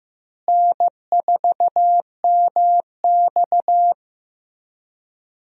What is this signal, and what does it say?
Morse code, heard as a keyed tone, about 15 wpm, 700 Hz. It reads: N4MX
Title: USA Call Signs - 4 Characters - Repeated 3x - 15wpm